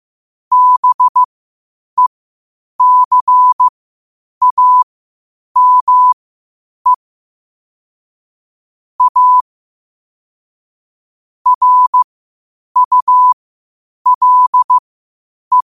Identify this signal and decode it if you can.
Morse code, heard as a keyed tone, about 15 wpm, spaced as for 8 wpm, 1000 Hz. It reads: BECAME A RULE
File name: qotdfiles / 8wpm